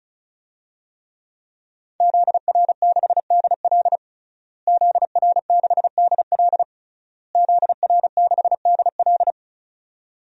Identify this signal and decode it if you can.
Morse code, heard as a keyed tone, about 36 wpm, 700 Hz. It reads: ZR6DL ZR6DL ZR6DL
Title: DXCC-100 Call Signs - 5 Characters - Repeated 3x - 35wpm